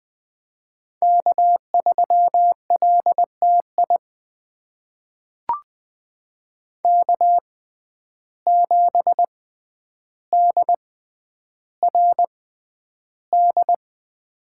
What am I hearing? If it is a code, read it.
Morse code, heard as a keyed tone, about 20 wpm, spaced as for 10 wpm, 700 Hz. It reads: K3LTI K7DRD
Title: USA Call Signs - 5 Characters - 6x Character Spacing - 20wpm